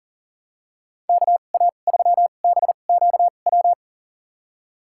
Morse code, heard as a keyed tone, about 40 wpm, spaced as for 29 wpm, 700 Hz. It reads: KA3BQW
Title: USA Call Signs - 6 Characters - 2x Character Spacing - 40wpm